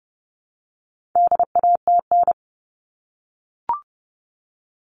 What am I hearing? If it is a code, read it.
Morse code, heard as a keyed tone, about 30 wpm, 700 Hz. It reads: DATN
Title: All Cut Numbers - 4-Digit 30wpm